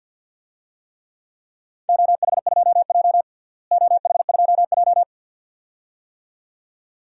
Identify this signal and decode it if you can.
Morse code, heard as a keyed tone, about 51 wpm, 700 Hz. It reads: OH2J OH2J
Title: DXCC-100 Call Signs - 4 Characters - Repeated 2x - 50wpm